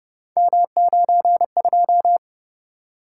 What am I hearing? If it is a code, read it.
Morse code, heard as a keyed tone, about 29 wpm, 700 Hz. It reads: M92